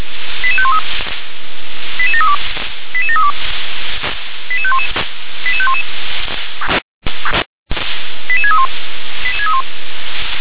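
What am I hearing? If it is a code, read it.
Five-tone selective call (ZVEI), heard as quick runs of five tones, tones of about 70 ms. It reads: 88513 88732 88732 88511 88511 88631
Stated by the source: ZVEI five-tone selective call received on SDR.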